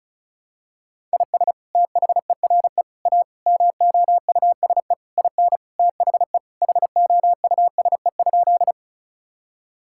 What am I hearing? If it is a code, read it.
Morse code, heard as a keyed tone, about 36 wpm, 700 Hz. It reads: IS THERE A MOUSE IN THE HOUSE?
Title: Sentences from Top 1000 Words - Review Part 5 - 35wpm